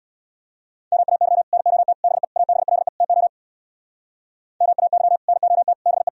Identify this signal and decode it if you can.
Morse code, heard as a keyed tone, about 63 wpm, 700 Hz. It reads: MAY NOT BE ABLE TO MAY NOT BE